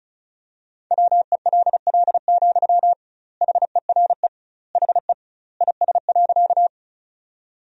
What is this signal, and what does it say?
Morse code, heard as a keyed tone, about 36 wpm, 700 Hz. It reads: WELL, HERE HE IS.
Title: 2x Word-Spacing - Sentences from Top 200 Words - 35wpm